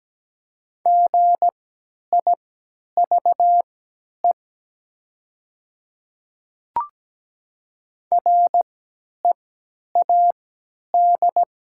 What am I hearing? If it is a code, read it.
Morse code, heard as a keyed tone, about 17 wpm, spaced as for 10 wpm, 700 Hz. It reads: GIVE READ
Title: 3x Character Spacing - Top 200 Words - 17wpm